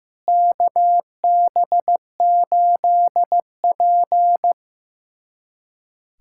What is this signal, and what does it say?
Morse code, heard as a keyed tone, about 15 wpm, 700 Hz. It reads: KB8P
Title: USA Callsign DE Callsign - 4 Characters - 15wpm